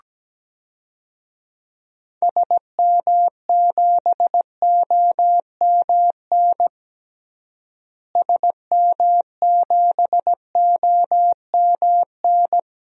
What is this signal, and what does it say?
Morse code, heard as a keyed tone, about 17 wpm, 700 Hz. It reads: SM7OMN SM7OMN
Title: DXCC-100 Call Signs - 6 Characters - Repeated 3x - 17wpm